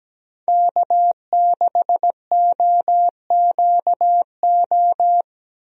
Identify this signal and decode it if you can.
Morse code, heard as a keyed tone, about 17 wpm, 700 Hz. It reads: K6OQO
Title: USA Call Signs - 5 Characters - Repeated 3x - 17wpm